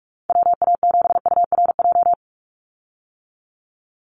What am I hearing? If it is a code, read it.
Morse code, heard as a keyed tone, about 44 wpm, 700 Hz. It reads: WA7URJ
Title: USA Call Signs 45wpm